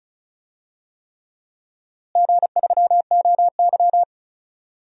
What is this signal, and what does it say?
Morse code, heard as a keyed tone, about 36 wpm, 700 Hz. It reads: G3OY
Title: DXCC-100 Call Signs - 4 Characters - Repeated 3x - 35wpm